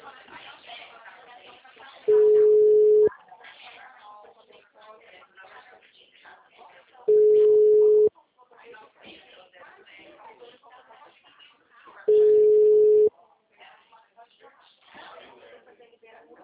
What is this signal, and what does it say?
Ringback tone (Europe, 425 Hz), with speech.
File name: out- -2032- - - .94199